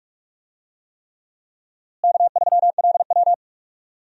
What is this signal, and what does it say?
Morse code, heard as a keyed tone, about 46 wpm, 700 Hz. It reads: K3LW
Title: USA Call Signs - 4 Characters - Repeated 3x - 45wpm